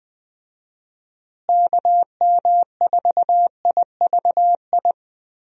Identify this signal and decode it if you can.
Morse code, heard as a keyed tone, about 20 wpm, 700 Hz. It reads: KM4IVI
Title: USA Call Signs - 6 Characters - Repeated 2x - 20wpm